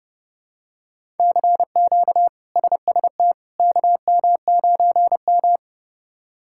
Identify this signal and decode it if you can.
Morse code, heard as a keyed tone, about 30 wpm, 700 Hz. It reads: CQ SST KM9M
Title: K1USN SST QSO Practice - 30wpm